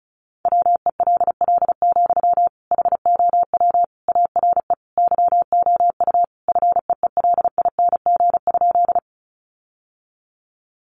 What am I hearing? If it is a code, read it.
Morse code, heard as a keyed tone, about 34 wpm, 700 Hz. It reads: WELL, HOW ARE YOU FEELING?
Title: Sentences from Top 1200 Words 35wpm